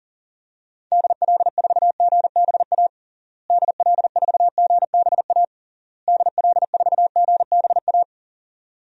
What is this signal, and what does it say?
Morse code, heard as a keyed tone, about 40 wpm, 700 Hz. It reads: DL4GBA DL4GBA DL4GBA
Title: DXCC-100 Call Signs - 6 Characters - Repeated 3x - 40wpm